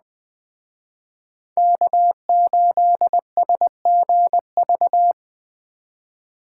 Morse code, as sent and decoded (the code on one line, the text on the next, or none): -.- ---.. ... --. ...-
K8SGV